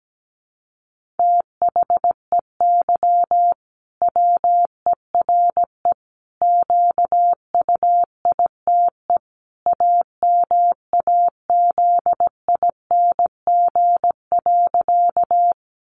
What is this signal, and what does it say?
Morse code, heard as a keyed tone, about 17 wpm, 700 Hz. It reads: THEY WERE QUITE AMAZING.
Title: Sentences from Top 1500 Words 17wpm